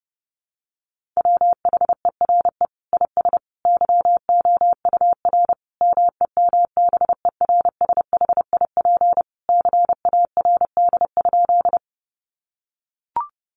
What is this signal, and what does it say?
Morse code, heard as a keyed tone, about 30 wpm, 700 Hz. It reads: WHERE IS YOUR MEMBERSHIP CARD?
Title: Sentences from Top 1400 Words 30wpm